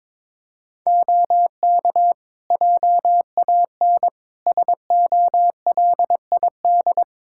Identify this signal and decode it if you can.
Morse code, heard as a keyed tone, about 22 wpm, 700 Hz. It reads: OK JAN SOLID
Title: K1USN SST QSO Practice 22wpm